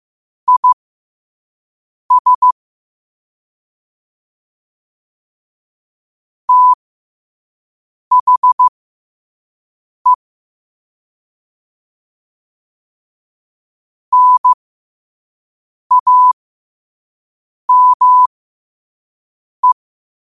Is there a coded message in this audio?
Yes: Morse code from 0.5 to 19.7 s, heard as a keyed tone, about 15 wpm, spaced as for 5 wpm, 1000 Hz. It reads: IS THE NAME